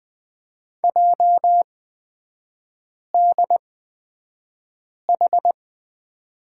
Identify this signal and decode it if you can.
Morse code, heard as a keyed tone, about 20 wpm, spaced as for 5 wpm, 700 Hz. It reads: JDH
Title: 224 Lesson 36e - BT - Three Characters 20wpm